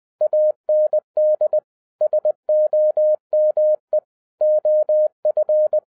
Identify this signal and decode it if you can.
Morse code, heard as a keyed tone, about 20 wpm, 610 Hz. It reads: AND SOME OF